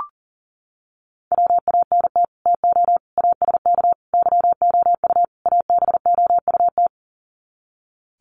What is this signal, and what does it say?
Morse code, heard as a keyed tone, about 40 wpm, 700 Hz. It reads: WANT TO ASK YOU ABOUT
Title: Sets of 5 Words Encore2 40wpm